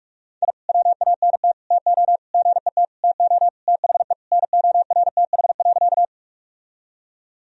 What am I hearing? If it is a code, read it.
Morse code, heard as a keyed tone, about 42 wpm, 700 Hz. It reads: I WANT TO GET TO THE NORTH.
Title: Sentences from Top 1000 Words - Review Part 1 - 45wpm